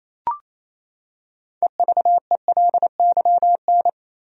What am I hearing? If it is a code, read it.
Morse code, heard as a keyed tone, about 28 wpm, 700 Hz. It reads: EVELYN
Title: USA Common Names - 28wpm